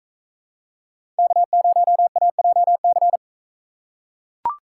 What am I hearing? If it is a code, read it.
Morse code, heard as a keyed tone, about 44 wpm, 700 Hz. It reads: K0AJC